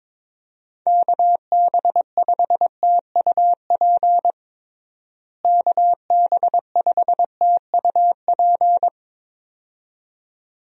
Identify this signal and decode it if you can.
Morse code, heard as a keyed tone, about 22 wpm, 700 Hz. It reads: KB5TUP KB5TUP
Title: USA Call Signs - 6 Characters - Repeated 2x - 22wpm